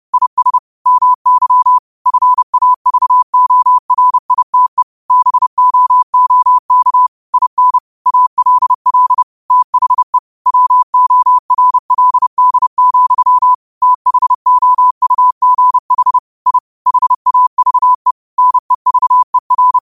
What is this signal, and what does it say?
Morse code, heard as a keyed tone, about 30 wpm, 1000 Hz. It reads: IS MY FAVORITE BOOK IN ALL THE WORLD, THOUGH I HAVE NEVER
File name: qotdfiles / 30wpm